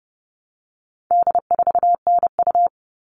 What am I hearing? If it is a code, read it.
Morse code, heard as a keyed tone, about 30 wpm, 700 Hz. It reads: D4NU
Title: All Cut Numbers - 4-Digit 30wpm